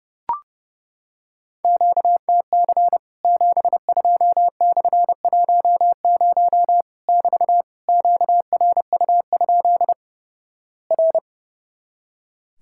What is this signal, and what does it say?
Morse code, heard as a keyed tone, about 32 wpm, 700 Hz. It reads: QTC 72/10 <BT> QRU? R
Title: Worked All Europe DX Contest - QTC Exchange - 30wpm